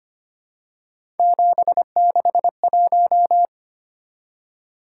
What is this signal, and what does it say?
Morse code, heard as a keyed tone, about 25 wpm, 700 Hz. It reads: 761
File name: Three Letter-Number Combinations - 25wpm